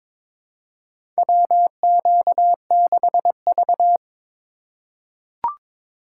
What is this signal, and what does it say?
Morse code, heard as a keyed tone, about 22 wpm, 700 Hz. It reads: WQ6V
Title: USA Call Signs - 4 Characters - 8x Character Spacing - 22wpm